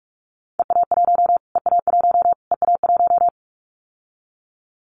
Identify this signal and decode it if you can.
Morse code, heard as a keyed tone, about 44 wpm, 700 Hz. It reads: EA1 EA1 EA1
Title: Callsign Prefixes for Top 25 DXCC - Repeated 3x - 45wpm